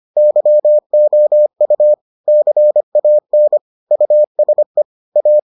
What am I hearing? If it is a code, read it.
Morse code, heard as a keyed tone, about 25 wpm, 600 Hz. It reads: YOU CAN USE A